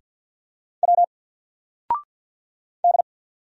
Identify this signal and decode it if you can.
Morse code, heard as a keyed tone, about 47 wpm, 700 Hz. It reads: W D
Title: Single Letter-Number Alt 50wpm